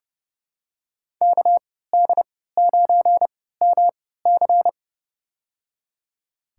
Morse code, heard as a keyed tone, about 32 wpm, spaced as for 17 wpm, 700 Hz. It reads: KD9MC
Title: USA Call Signs - 5 Characters - 3x Character Spacing - 30wpm